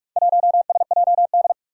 Morse code, heard as a keyed tone, about 45 wpm, 700 Hz. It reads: 1SJD
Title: Four Challenging Letter-Number Combinations 45wpm